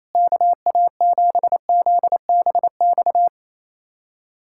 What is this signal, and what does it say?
Morse code, heard as a keyed tone, about 28 wpm, 700 Hz. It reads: KA7ZBX
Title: USA Callsign DE Callsign - 6 Characters - 28wpm